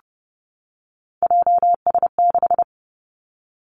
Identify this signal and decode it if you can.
Morse code, heard as a keyed tone, about 30 wpm, 700 Hz. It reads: JS6
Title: Three Challenging Letter-Number Combinations 30wpm